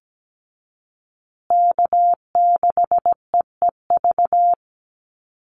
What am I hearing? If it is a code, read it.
Morse code, heard as a keyed tone, about 17 wpm, 700 Hz. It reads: K6EEV
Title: USA Call Signs Repeated 2x 17wpm